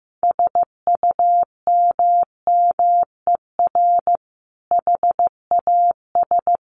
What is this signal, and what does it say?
Morse code, heard as a keyed tone, about 15 wpm, 700 Hz. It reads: SUMMER HAS
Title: Sentences from Top 1100 Words 15wpm